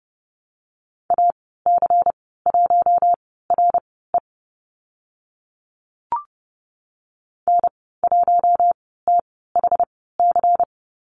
Morse code, heard as a keyed tone, about 30 wpm, spaced as for 17 wpm, 700 Hz. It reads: AC1RE N1THC
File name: Lesson 19g - Letter w - Call Signs 30wpm